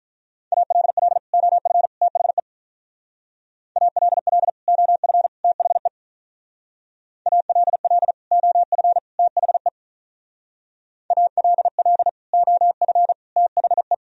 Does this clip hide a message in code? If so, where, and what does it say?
Yes: Morse code from 0.5 to 13.9 s, heard as a keyed tone, about 44 wpm, 700 Hz. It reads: ALL OF THE ALL OF THE ALL OF THE ALL OF THE